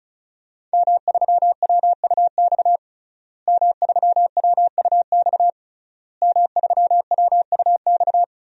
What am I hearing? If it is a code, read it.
Morse code, heard as a keyed tone, about 34 wpm, 700 Hz. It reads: M3WUX M3WUX M3WUX
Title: DXCC-100 Call Signs - 5 Characters - Repeated 3x - 35wpm